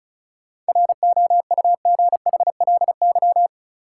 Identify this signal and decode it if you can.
Morse code, heard as a keyed tone, about 34 wpm, 700 Hz. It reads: ROUGHLY
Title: Sentences from Top 1500 Words - New Word Drill Alt - 35wpm